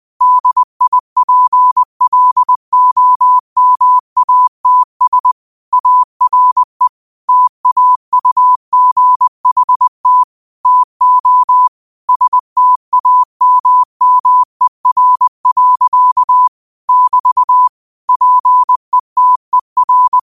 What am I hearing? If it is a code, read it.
Morse code, heard as a keyed tone, about 20 wpm, 1000 Hz. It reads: DIPLOMATS ARE TAUGHT TO STAMMER. <BT> PETER
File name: qotdfiles / 20wpm